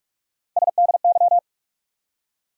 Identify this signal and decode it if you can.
Morse code, heard as a keyed tone, about 46 wpm, 700 Hz. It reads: SDY
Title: USA Call Sign Postfixes - 3 Letters 45wpm